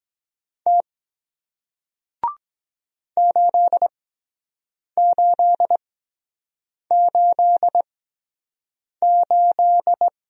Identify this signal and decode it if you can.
Morse code, heard as a keyed tone, about 23 wpm, spaced as for 7 wpm, 700 Hz. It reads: T 8888